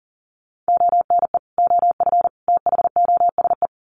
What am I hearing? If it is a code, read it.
Morse code, heard as a keyed tone, about 40 wpm, 700 Hz. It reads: ONE OF THOSE
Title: Sets of 3 Words 40wpm